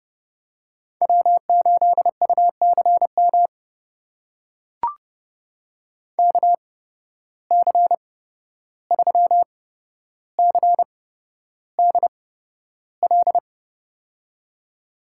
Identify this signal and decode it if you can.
Morse code, heard as a keyed tone, about 30 wpm, spaced as for 12 wpm, 700 Hz. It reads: W8UCM KC3CDL
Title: USA Call Signs - 8x Character Spacing - 30wpm